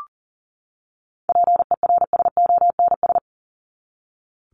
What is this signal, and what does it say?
Morse code, heard as a keyed tone, about 40 wpm, 700 Hz. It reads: PERSONS
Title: ES:S Suffix - Top 100 Words with Base Words 40wpm